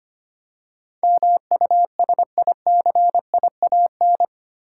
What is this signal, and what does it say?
Morse code, heard as a keyed tone, about 25 wpm, 700 Hz. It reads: MUSICIAN
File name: 8 Letter Words - First 3 Letters Given - 25wpm